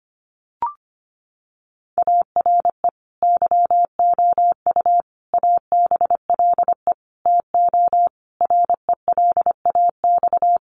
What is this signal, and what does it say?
Morse code, heard as a keyed tone, about 25 wpm, 700 Hz. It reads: ARE YOU ABLE TO RELAX
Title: Sentences from Top 1400 Words 25wpm